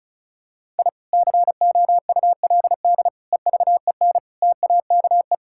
Morse code, heard as a keyed tone, about 34 wpm, 690 Hz. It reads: I COULD EVEN TAKE
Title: Sentences from Top 1000 Words - Review Part 5 - 35wpm